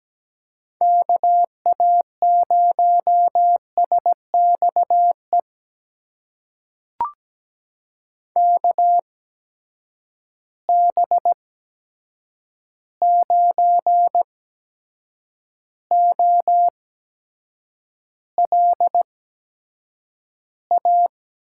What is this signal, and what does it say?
Morse code, heard as a keyed tone, about 17 wpm, spaced as for 7 wpm, 700 Hz. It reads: KA0SXE KB9OLA
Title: USA Call Signs - 6 Characters - 8x Character Spacing - 17wpm